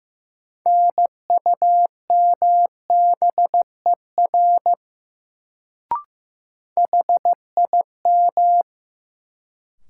Morse code, heard as a keyed tone, about 15 wpm, 700 Hz. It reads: NUMBER HIM
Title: Sentences from Top 100 Words - New Word Drill - 15wpm